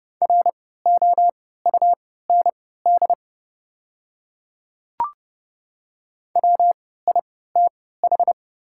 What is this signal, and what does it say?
Morse code, heard as a keyed tone, about 30 wpm, spaced as for 17 wpm, 700 Hz. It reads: ROUND WITH
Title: 3x Character Spacing - Top 200 Words - 30wpm